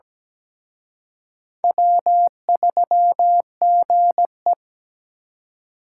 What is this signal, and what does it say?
Morse code, heard as a keyed tone, about 20 wpm, 700 Hz. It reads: W3GE
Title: Speed Racing - USA 1x2 Call Signs 17wpm